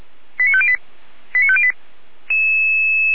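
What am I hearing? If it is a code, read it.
Five-tone selective call (ZVEI), heard as quick runs of five tones, tones of about 70 ms. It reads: 89508 89508
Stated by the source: ZVEI five-tone selective call 89160 received via SDR, on the feuerwehr-orlamuende.org fire brigade server.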